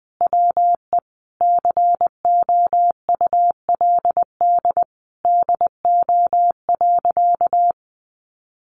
Morse code, heard as a keyed tone, about 20 wpm, 700 Hz. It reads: WE COULD DO.
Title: Sentences from Top 1400 Words 20wpm